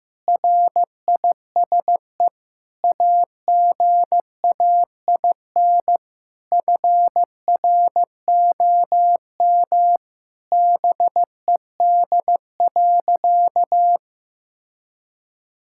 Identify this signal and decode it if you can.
Morse code, heard as a keyed tone, about 15 wpm, 700 Hz. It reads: RISE AGAIN FROM BED.
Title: Sentences from Top 1000 Words - Review Part 6 - 15wpm